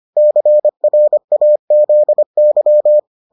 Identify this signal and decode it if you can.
Morse code, heard as a keyed tone, about 25 wpm, 600 Hz. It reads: CRAZY